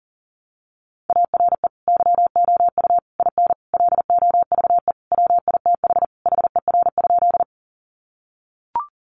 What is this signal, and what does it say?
Morse code, heard as a keyed tone, about 37 wpm, 700 Hz. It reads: ARE YOU IN LOVE WITH HER?
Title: Sentences from Top 400 Words 40wpm